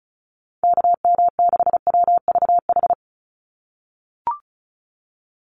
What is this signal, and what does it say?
Morse code, heard as a keyed tone, about 34 wpm, 700 Hz. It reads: KM6WVH
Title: USA Call Signs 35wpm